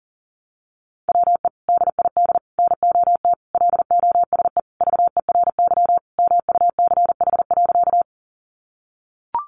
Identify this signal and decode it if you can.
Morse code, heard as a keyed tone, about 40 wpm, 700 Hz. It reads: WE DID NOT LOSE VERY MUCH.
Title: Sentences from Top 1100 Words 40wpm